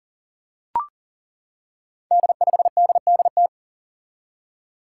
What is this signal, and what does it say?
Morse code, heard as a keyed tone, about 38 wpm, 690 Hz. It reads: D5DDT
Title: Five Challenging Letter-Number Combinations 40wpm